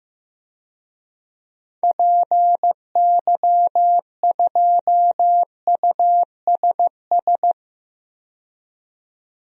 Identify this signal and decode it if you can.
Morse code, heard as a keyed tone, about 16 wpm, 700 Hz. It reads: PY2USS
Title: DXCC-100 Call Signs - 6 Characters - 15wpm